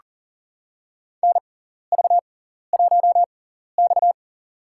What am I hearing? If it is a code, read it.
Morse code, heard as a keyed tone, about 40 wpm, spaced as for 14 wpm, 700 Hz. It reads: NV1X
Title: USA Call Signs - 6x Character Spacing - 40wpm